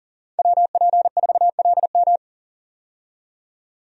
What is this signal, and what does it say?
Morse code, heard as a keyed tone, about 40 wpm, 700 Hz. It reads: WP4LM
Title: USA Call Signs - 5 Characters - 40wpm